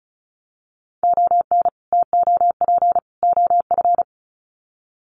Morse code, heard as a keyed tone, about 34 wpm, 700 Hz. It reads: ON TOP OF
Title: Sets of 3 Words 35wpm